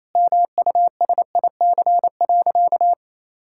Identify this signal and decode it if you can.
Morse code, heard as a keyed tone, about 28 wpm, 700 Hz. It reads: MUSIC.
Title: Sentences from Top 1600 Words 28wpm